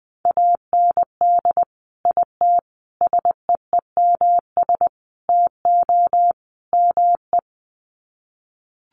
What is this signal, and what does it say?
Morse code, heard as a keyed tone, about 20 wpm, 700 Hz. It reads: AND IT SEEMS TO ME
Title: Sets of 5 Words Encore2 20wpm